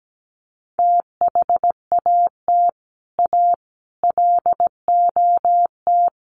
Morse code, heard as a keyed tone, about 17 wpm, 700 Hz. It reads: THAT A LOT
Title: Sets of 4 Words 17wpm